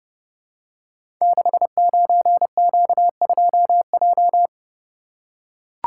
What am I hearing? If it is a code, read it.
Morse code, heard as a keyed tone, about 28 wpm, 700 Hz. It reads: 69Q2J
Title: Code Groups - 5 Letter:Numbers - 30wpm